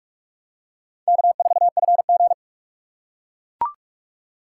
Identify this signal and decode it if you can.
Morse code, heard as a keyed tone, about 43 wpm, 690 Hz. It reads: K4FG
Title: USA Call Signs - 4x Character Spacing - 45wpm